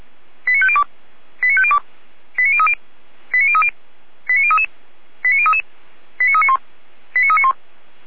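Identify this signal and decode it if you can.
Five-tone selective call (ZVEI), heard as quick runs of five tones, tones of about 70 ms. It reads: 89502 89502 89030 89030 89033 89033 89391 89391
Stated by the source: Five-tone ZVEI selective call received on an SDR.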